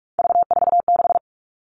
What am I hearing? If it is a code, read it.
Morse code, heard as a keyed tone, about 44 wpm, 700 Hz. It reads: V46
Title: All Cut Numbers - 3-Digit 45wpm